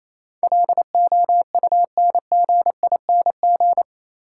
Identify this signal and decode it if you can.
Morse code, heard as a keyed tone, about 27 wpm, 690 Hz. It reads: LOUNGING
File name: Easy Words - Speed-Racing with Hints - 22wpm